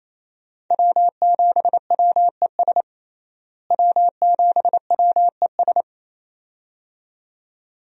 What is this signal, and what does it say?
Morse code, heard as a keyed tone, about 28 wpm, 700 Hz. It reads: W7WES W7WES
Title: USA Call Signs - 5 Characters - Repeated 2x - 28wpm